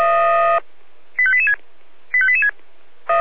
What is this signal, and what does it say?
Five-tone selective call (ZVEI), heard as quick runs of five tones, tones of about 70 ms. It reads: 85886 85886